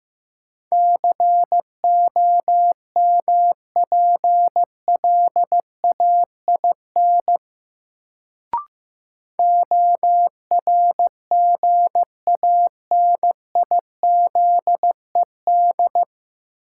Morse code, heard as a keyed tone, about 15 wpm, 700 Hz. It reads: COMPLAIN ORGANIZED
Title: Sentences from Top 1300 Words - New Word Drill Alt - 15wpm